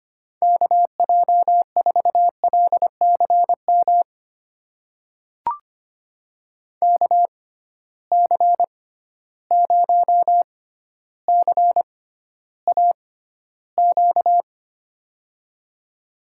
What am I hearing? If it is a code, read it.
Morse code, heard as a keyed tone, about 25 wpm, spaced as for 13 wpm, 700 Hz. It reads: KJ4LCM KC0CAQ
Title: USA Call Signs - 6 Characters - 6x Character Spacing - 25wpm